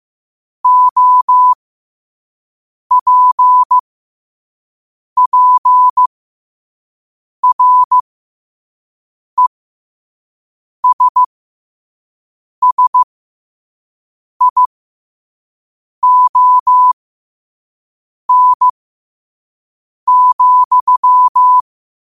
Morse code, heard as a keyed tone, about 15 wpm, spaced as for 5 wpm, 1000 Hz. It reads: OPPRESSION,
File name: qotdfiles / 5wpm